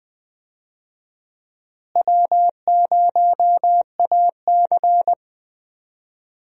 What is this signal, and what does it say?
Morse code, heard as a keyed tone, about 20 wpm, 700 Hz. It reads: W0AC
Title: USA Call Signs - 4 Characters - Repeated 2x - 20wpm